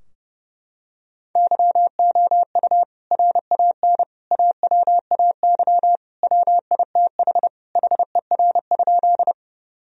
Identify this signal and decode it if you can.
Morse code, heard as a keyed tone, about 32 wpm, 700 Hz. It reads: YOU RAN AWAY WITH HER?
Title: Sentences from Top 1000 Words - Review Part 3 - 30wpm